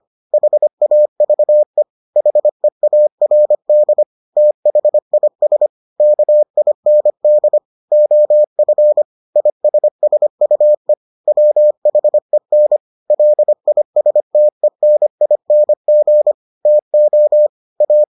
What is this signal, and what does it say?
Morse code, heard as a keyed tone, about 25 wpm, 600 Hz. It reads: HAVE HEARD THIS KIND OF ISSUE WHEN LISTENING TO A